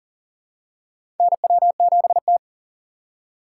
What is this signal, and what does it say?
Morse code, heard as a keyed tone, about 40 wpm, 700 Hz. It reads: NW7T
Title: USA Call Signs - 4 Characters - 2x Character Spacing - 40wpm